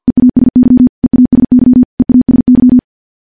Morse code, heard as a keyed tone, about 50 wpm, 250 Hz. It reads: EASY EASY EASY
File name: morse-training-50WPM